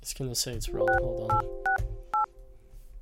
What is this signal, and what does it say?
DTMF, heard as telephone touch tones, tones of about 130 ms, with speech. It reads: 6868